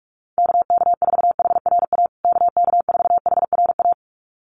Morse code, heard as a keyed tone, about 44 wpm, 700 Hz. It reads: KK4HRA KK4HRA
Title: USA Call Signs Repeated 2x 45wpm